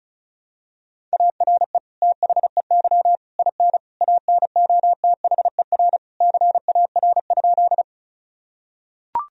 Morse code, heard as a keyed tone, about 34 wpm, 700 Hz. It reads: ARE THEY IN ANOTHER CAR?
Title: Sentences from Top 1000 Words - Review Part 1 - 35wpm